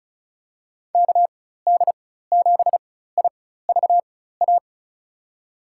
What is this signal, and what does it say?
Morse code, heard as a keyed tone, about 34 wpm, spaced as for 16 wpm, 700 Hz. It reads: KD7IVA
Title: USA Call Signs - 6 Characters - 4x Character Spacing - 35wpm